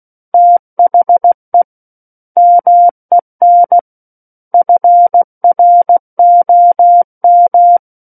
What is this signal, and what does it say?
Morse code, heard as a keyed tone, about 16 wpm, 700 Hz. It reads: THE MEN FROM
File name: Book One / 07-Chapter 7 - 16 WPM